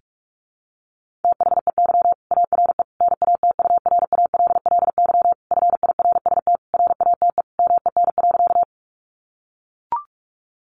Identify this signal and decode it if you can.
Morse code, heard as a keyed tone, about 44 wpm, 700 Hz. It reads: THEY ARE NATURALLY FIRST RATE MEN.
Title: Sentences from Top 1300 Words 45wpm